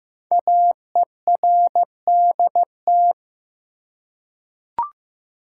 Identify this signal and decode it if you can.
Morse code, heard as a keyed tone, about 15 wpm, 700 Hz. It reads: AERDT
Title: Code Groups - 5 Letters - 15wpm